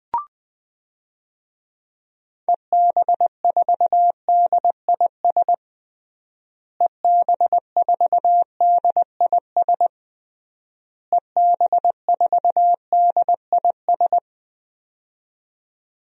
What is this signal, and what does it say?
Morse code, heard as a keyed tone, about 21 wpm, 700 Hz. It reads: EB4DIS EB4DIS EB4DIS
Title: DXCC-100 Call Signs - 6 Characters - Repeated 3x - 20wpm